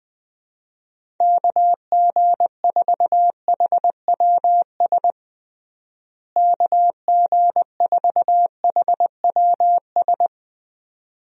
Morse code, heard as a keyed tone, about 20 wpm, 700 Hz. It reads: KG4HWS KG4HWS
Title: USA Call Signs - 6 Characters - Repeated 2x - 20wpm